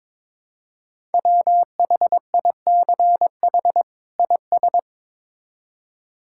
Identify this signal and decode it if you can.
Morse code, heard as a keyed tone, about 22 wpm, 700 Hz. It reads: WHICH IS
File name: Build A Phrase - 2-Word Phrases - 22wpm